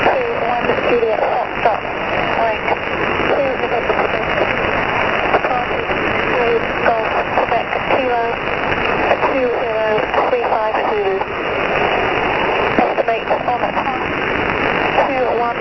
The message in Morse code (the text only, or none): none